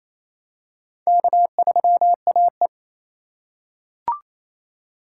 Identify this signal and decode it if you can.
Morse code, heard as a keyed tone, about 27 wpm, 700 Hz. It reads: K3AE
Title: USA Call Signs - 4 Characters - 4x Character Spacing - 28wpm